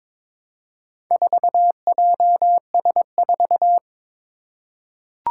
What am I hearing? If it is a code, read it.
Morse code, heard as a keyed tone, about 22 wpm, 700 Hz. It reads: 4JS4
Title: Four Challenging Letter-Number Combinations 22wpm